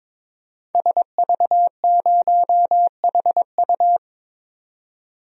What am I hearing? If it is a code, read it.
Morse code, heard as a keyed tone, about 21 wpm, 700 Hz. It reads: SV0HU
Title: DXCC-100 Call Signs - 5 Characters - 22wpm